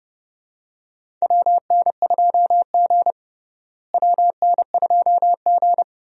Morse code, heard as a keyed tone, about 32 wpm, 700 Hz. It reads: WN2G WN2G
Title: USA Call Signs - 4 Characters - Repeated 2x - 30wpm